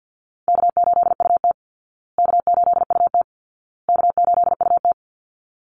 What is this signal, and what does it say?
Morse code, heard as a keyed tone, about 50 wpm, 700 Hz. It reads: K8UT K8UT K8UT
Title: Easy Call Signs Repeated 3x - 50wpm